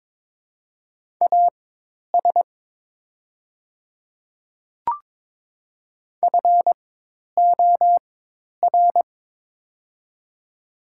Morse code, heard as a keyed tone, about 22 wpm, spaced as for 10 wpm, 700 Hz. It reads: AS FOR
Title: 4x Character Spacing - Top 100 Words - 22wpm